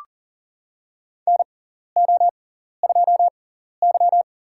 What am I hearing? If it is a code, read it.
Morse code, heard as a keyed tone, about 40 wpm, spaced as for 14 wpm, 700 Hz. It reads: NO2Y
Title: USA Call Signs - 4 Characters - 6x Character Spacing - 40wpm